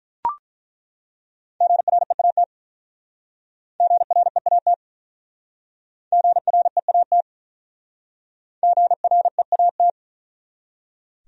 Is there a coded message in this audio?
Yes: Morse code from 1.6 to 9.9 s, heard as a keyed tone, about 47 wpm, 700 Hz. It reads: GREAT GREAT GREAT GREAT